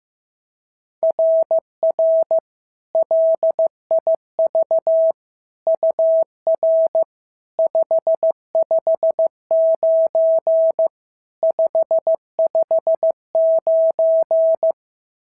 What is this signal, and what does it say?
Morse code, heard as a keyed tone, about 14 wpm, 650 Hz. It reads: RR LIV UR 559 559
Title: SOTA QSO Practice 15wpm